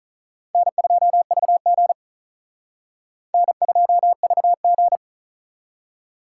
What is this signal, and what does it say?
Morse code, heard as a keyed tone, about 43 wpm, 700 Hz. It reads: N2VG N2VG
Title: Speed Racing - USA 1x2 Call Signs 35wpm